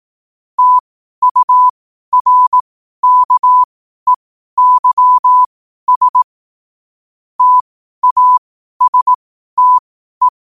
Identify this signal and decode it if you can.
Morse code, heard as a keyed tone, about 18 wpm, spaced as for 12 wpm, 1000 Hz. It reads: TURKEYS TASTE
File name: qotdfiles / 12wpm